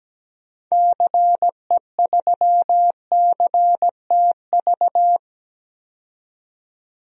Morse code, heard as a keyed tone, about 17 wpm, 700 Hz. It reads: CE3CTV
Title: DXCC-100 Call Signs - 6 Characters - 17wpm